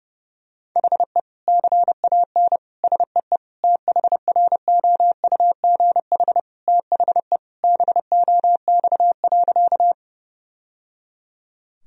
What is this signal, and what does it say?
Morse code, heard as a keyed tone, about 30 wpm, 700 Hz. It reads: HE CAN SEE THROUGH THE BOX.
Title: Sentences from Top 1000 Words - Review Part 2 - 30wpm